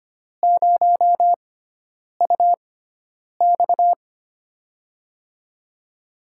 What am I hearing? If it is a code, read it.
Morse code, heard as a keyed tone, about 25 wpm, spaced as for 9 wpm, 700 Hz. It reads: 0UX
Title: USA Call Signs - 4 Characters - 6x Character Spacing - 25wpm